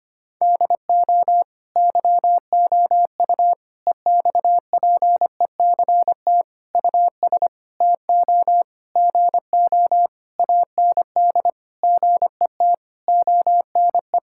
Morse code, heard as a keyed tone, about 25 wpm, 700 Hz. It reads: DO YOU EXPECT US TO GO AND GET ONE
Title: Sentences from Top 1600 Words 25wpm